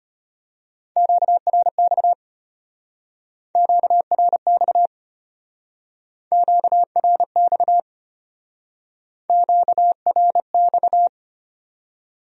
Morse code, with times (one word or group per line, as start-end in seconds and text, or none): 1.0-2.1 QRX
3.6-4.9 QRX
6.3-7.8 QRX
9.3-11.1 QRX